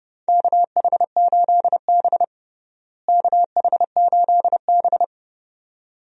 Morse code, heard as a keyed tone, about 32 wpm, 700 Hz. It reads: KH8B KH8B
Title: USA Call Signs - 4 Characters - Repeated 2x - 30wpm